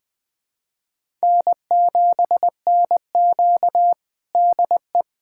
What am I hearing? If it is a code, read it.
Morse code, heard as a keyed tone, about 21 wpm, 700 Hz. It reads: N7NQ DE
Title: USA Callsign DE Callsign - 4 Characters - 20wpm